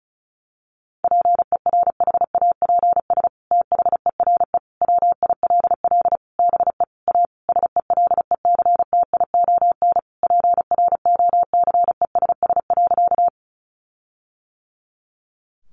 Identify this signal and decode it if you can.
Morse code, heard as a keyed tone, about 34 wpm, 700 Hz. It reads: PERHAPS THERE WILL BE A SELECTION PROCESS.
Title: Sentences from Top 1400 Words 35wpm